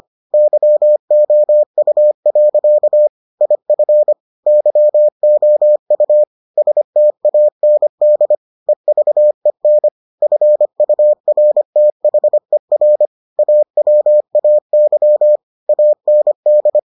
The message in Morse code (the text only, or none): YOU. IF YOU STAND EVEN FURTHER AWAY AND